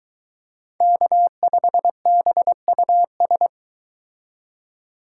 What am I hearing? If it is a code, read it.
Morse code, heard as a keyed tone, about 22 wpm, 700 Hz. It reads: K5BUS